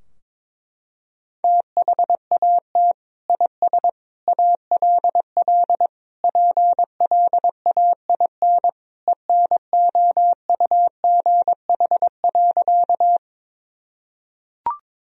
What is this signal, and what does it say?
Morse code, heard as a keyed tone, about 22 wpm, 700 Hz. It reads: THAT IS ALL PLAIN ENOUGH.
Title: Sentences from Top 1000 Words - Review Part 2 - 22wpm